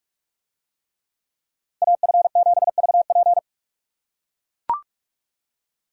Morse code, heard as a keyed tone, about 44 wpm, 700 Hz. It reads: AF7VP